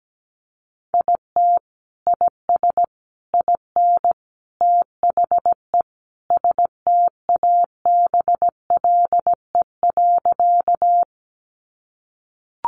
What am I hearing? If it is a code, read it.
Morse code, heard as a keyed tone, about 17 wpm, 700 Hz. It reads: IT IS IN THE STABLE.
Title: Sentences from Top 1500 Words 17wpm